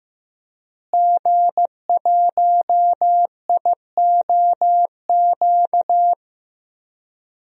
Morse code, heard as a keyed tone, about 15 wpm, 700 Hz. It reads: G1IOQ
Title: DXCC-100 Call Signs - 5 Characters - Repeated 3x - 15wpm